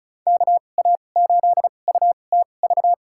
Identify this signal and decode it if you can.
Morse code, heard as a keyed tone, about 36 wpm, spaced as for 25 wpm, 700 Hz. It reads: KA8UTV
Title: USA Call Signs - 6 Characters - 2x Character Spacing - 35wpm